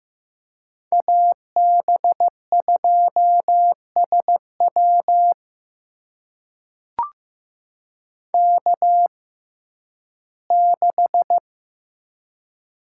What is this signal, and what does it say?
Morse code, heard as a keyed tone, about 15 wpm, spaced as for 11 wpm, 700 Hz. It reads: AB2SW K6
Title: USA Call Signs - 6x Character Spacing - 15wpm